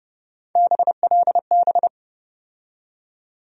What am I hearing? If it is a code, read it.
Morse code, heard as a keyed tone, about 31 wpm, 700 Hz. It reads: BLB
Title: USA Call Sign Postfixes - 3 Letters 30wpm